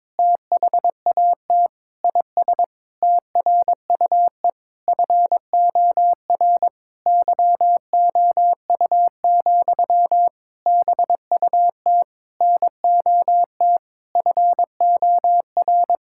Morse code, heard as a keyed tone, about 22 wpm, 700 Hz. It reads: THAT IS TRUE FOR YOU, BUT NOT FOR
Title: Sentences from Top 1000 Words - Review Part 5 - 22wpm